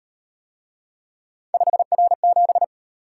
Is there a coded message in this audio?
Yes: Morse code from 1.5 to 2.6 s, heard as a keyed tone, about 38 wpm, 700 Hz. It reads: 5R7